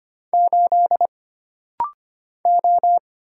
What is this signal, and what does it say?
Morse code, heard as a keyed tone, about 25 wpm, 700 Hz. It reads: 8 O
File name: Single Character (Letter:Number:Punctuation:Prosign) - Rapid-Fire - 25wpm